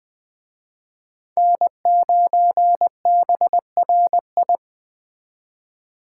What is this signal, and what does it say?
Morse code, heard as a keyed tone, about 21 wpm, 700 Hz. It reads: N9BRI
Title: Speed Racing - USA Call Signs 17wpm